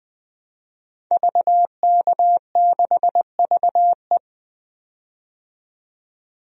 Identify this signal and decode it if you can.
Morse code, heard as a keyed tone, about 20 wpm, 700 Hz. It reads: VK6VE
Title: DXCC-100 Call Signs - 5 Characters - 20wpm